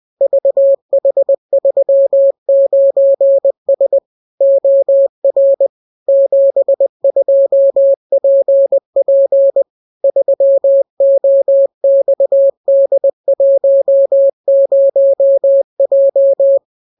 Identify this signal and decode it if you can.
Morse code, heard as a keyed tone, about 20 wpm, 550 Hz. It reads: VH39S OR 72PP 3OXD10J